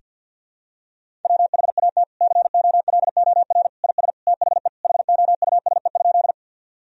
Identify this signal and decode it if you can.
Morse code, heard as a keyed tone, about 52 wpm, 700 Hz. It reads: WHAT COLOR IS THE HOUSE?
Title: Sentences from Top 1000 Words - Review Part 2 - 50wpm